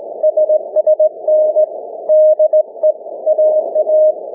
Morse code, heard as a keyed tone, about 21 wpm, 620 Hz. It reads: SSN DE AA
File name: cw / ssn 30M